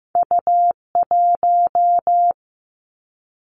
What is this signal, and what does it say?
Morse code, heard as a keyed tone, about 15 wpm, 700 Hz. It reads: U1
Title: Two Challenging Letter-Number Combinations 15wpm